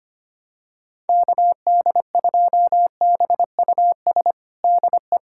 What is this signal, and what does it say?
Morse code, heard as a keyed tone, about 25 wpm, 700 Hz. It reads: KD2BUS DE
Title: USA Callsign DE Callsign 25wpm